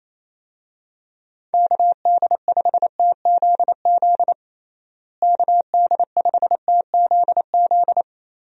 Morse code, heard as a keyed tone, about 28 wpm, 700 Hz. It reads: KD5TZZ KD5TZZ
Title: USA Call Signs - 6 Characters - Repeated 2x - 28wpm